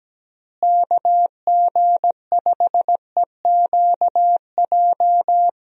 Morse code, heard as a keyed tone, about 17 wpm, 700 Hz. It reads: KG5EQJ
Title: USA Call Signs - 6 Characters - Repeated 3x - 17wpm